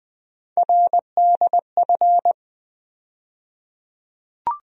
Morse code, heard as a keyed tone, about 20 wpm, 700 Hz. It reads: RDF
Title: USA Call Sign Postfixes - 3 Letters 20wpm